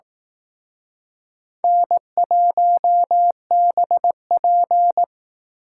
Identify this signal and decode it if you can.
Morse code, heard as a keyed tone, about 18 wpm, 700 Hz. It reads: N1BP
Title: Speed Racing - USA 1x2 Call Signs 15wpm